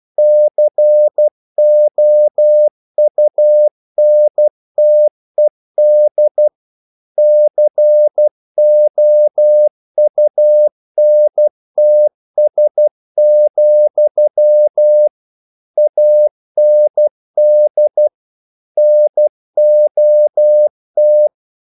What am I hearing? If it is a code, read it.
Morse code, heard as a keyed tone, about 12 wpm, 600 Hz. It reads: COUNTED COUNTS, AND NOT
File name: CW / Practice / EnglishQuotations0112